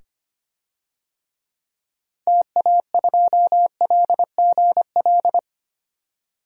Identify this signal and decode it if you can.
Morse code, heard as a keyed tone, about 25 wpm, 700 Hz. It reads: TA2LGL
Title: DXCC-100 Call Signs - 6 Characters - 25wpm